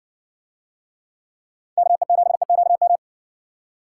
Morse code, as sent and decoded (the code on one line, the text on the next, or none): -.- . --... --.- --
KE7EQM